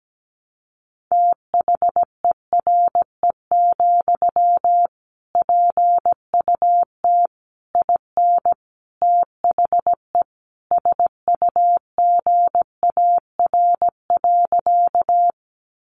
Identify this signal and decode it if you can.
Morse code, heard as a keyed tone, about 17 wpm, 700 Hz. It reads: THERE, PUT IN THE SUGAR.
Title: Sentences from Top 1200 Words 17wpm